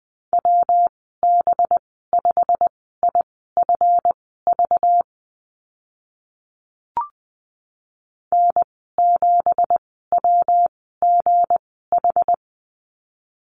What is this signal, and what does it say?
Morse code, heard as a keyed tone, about 20 wpm, spaced as for 14 wpm, 700 Hz. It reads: WB5IFV N7WGH
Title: 193 Lesson 31g - Letter v - Call Signs 20wpm